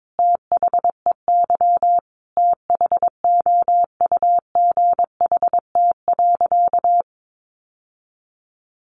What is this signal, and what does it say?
Morse code, heard as a keyed tone, about 22 wpm, 700 Hz. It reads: THEY THOUGHT.
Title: Sentences from Top 800 Words 22wpm